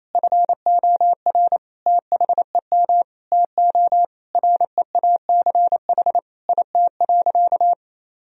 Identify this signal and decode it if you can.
Morse code, heard as a keyed tone, about 28 wpm, 700 Hz. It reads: FOR THEM TO REACH IT.
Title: Sentences from Top 1000 Words - Review Part 2 - 28wpm